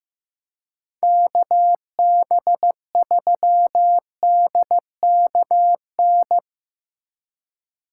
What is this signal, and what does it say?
Morse code, heard as a keyed tone, about 17 wpm, 700 Hz. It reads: KB3DKN
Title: Speed Racing - USA 2x3 Call Signs 15wpm